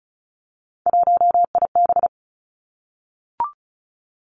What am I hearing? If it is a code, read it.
Morse code, heard as a keyed tone, about 36 wpm, 700 Hz. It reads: 1IB
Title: Three Challenging Letter-Number Combinations 35wpm